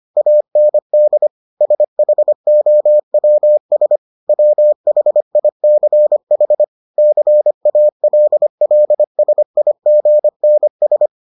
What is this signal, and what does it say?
Morse code, heard as a keyed tone, about 25 wpm, 600 Hz. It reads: AND SHOWS WHICH CALLSIGNS